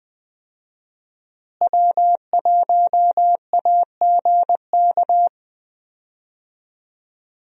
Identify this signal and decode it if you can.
Morse code, heard as a keyed tone, about 20 wpm, 700 Hz. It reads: W1AGK
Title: USA Call Signs - 5 Characters - Repeated 2x - 20wpm